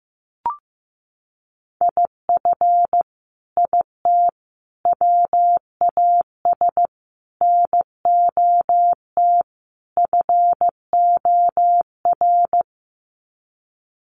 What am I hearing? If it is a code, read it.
Morse code, heard as a keyed tone, about 15 wpm, 700 Hz. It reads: IF IT WAS NOT FOR
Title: Sets of 5 Words Encore 15wpm